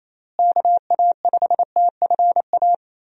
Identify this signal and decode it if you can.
Morse code, heard as a keyed tone, about 27 wpm, 700 Hz. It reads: KA5TFA
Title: USA Call Signs - 6 Characters - Repeated 2x - 28wpm